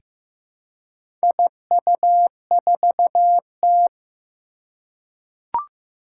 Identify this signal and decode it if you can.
Morse code, heard as a keyed tone, about 15 wpm, 700 Hz. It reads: IU4T
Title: Four Challenging Letter-Number Combinations 15wpm